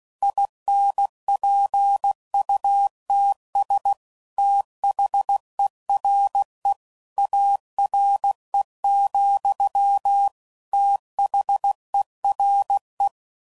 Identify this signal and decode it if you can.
Morse code, heard as a keyed tone, about 16 wpm, 800 Hz. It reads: INPUTS THERE ARE, THERE
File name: CW Code Groups @15WPM / Technical Subjects / AND Gate